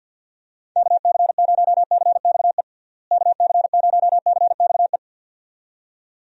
Morse code, heard as a keyed tone, about 51 wpm, 700 Hz. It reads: KC0CXE KC0CXE
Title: USA Call Signs - 6 Characters - Repeated 2x - 50wpm